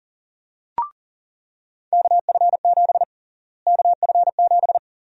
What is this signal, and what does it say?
Morse code, heard as a keyed tone, about 37 wpm, 700 Hz. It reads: KF7 KF7
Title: USA Call Sign Prefixes - Repeated 2x - 40wpm